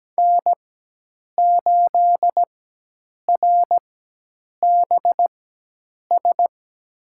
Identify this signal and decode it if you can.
Morse code, heard as a keyed tone, about 17 wpm, spaced as for 8 wpm, 700 Hz. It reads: N8RBS
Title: USA Call Signs - 4x Character Spacing - 17wpm